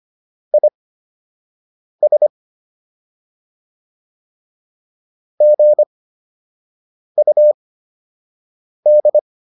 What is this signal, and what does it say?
Morse code, heard as a keyed tone, about 25 wpm, spaced as for 6 wpm, 600 Hz. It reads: IS GUD